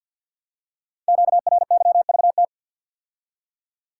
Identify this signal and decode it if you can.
Morse code, heard as a keyed tone, about 51 wpm, 700 Hz. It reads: QRYVT
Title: Code Groups - 5 Letter:Numbers - 50wpm